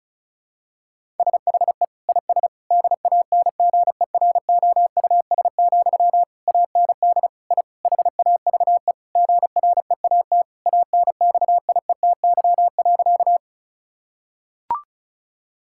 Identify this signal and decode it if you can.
Morse code, heard as a keyed tone, about 36 wpm, 700 Hz. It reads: SHE IS DANGEROUS, AND I HAVE GREAT ANXIETY.
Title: Sentences from Top 1600 Words 35wpm